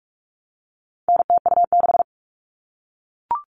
Morse code, heard as a keyed tone, about 44 wpm, 700 Hz. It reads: NTU6
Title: All Cut Numbers - 4-Digit 45wpm